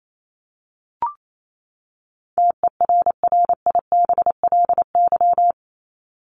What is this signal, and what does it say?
Morse code, heard as a keyed tone, about 28 wpm, 700 Hz. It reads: TERRIBLY
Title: Top 1751-2000 Words 28wpm